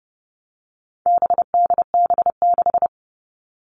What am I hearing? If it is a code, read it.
Morse code, heard as a keyed tone, about 30 wpm, 700 Hz. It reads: BDB6
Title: All Cut Numbers - 4-Digit 30wpm